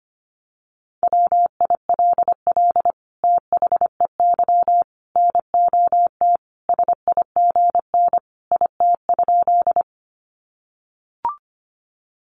Morse code, heard as a keyed tone, about 25 wpm, 700 Hz. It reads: WILL THEY NOT SIGN IT?
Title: Sentences from Top 1000 Words 25wpm